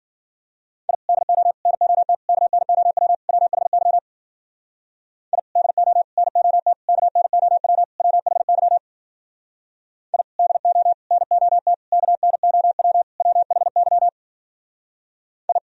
Morse code, heard as a keyed tone, about 55 wpm, 700 Hz. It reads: I DO NOT KNOW WHY I DO NOT KNOW WHY I DO NOT KNOW WHY I